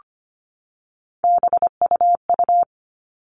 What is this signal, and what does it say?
Morse code, heard as a keyed tone, about 25 wpm, 700 Hz. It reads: BUU
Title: All Cut Numbers - 3-Digit 25wpm